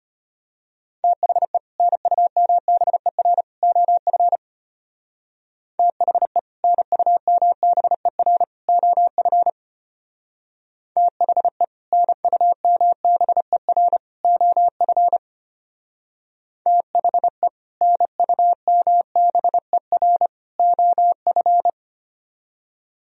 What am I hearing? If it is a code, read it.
Morse code, heard as a keyed tone, about 31 wpm, 700 Hz. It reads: THE NUMBER OF THE NUMBER OF THE NUMBER OF THE NUMBER OF
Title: Speed-Racing - Sets of 3 Words 25wpm